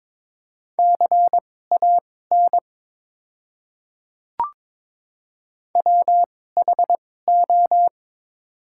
Morse code, heard as a keyed tone, about 22 wpm, spaced as for 16 wpm, 700 Hz. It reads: CAN WHO
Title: 2x Character Spacing - Top 100 Words - 22wpm